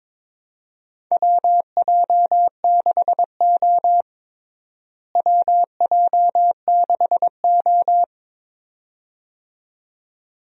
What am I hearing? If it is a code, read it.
Morse code, heard as a keyed tone, about 22 wpm, 700 Hz. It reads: WJ6O WJ6O
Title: USA Call Signs - 4 Characters - Repeated 2x - 22wpm